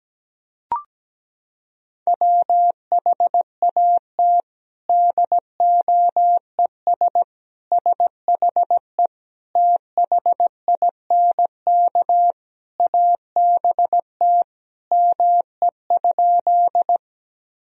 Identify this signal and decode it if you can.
Morse code, heard as a keyed tone, about 17 wpm, 700 Hz. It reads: WHAT DOES SHE THINK ABT ME?
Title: Sentences from Top 200 Words and Abbreviations 17wpm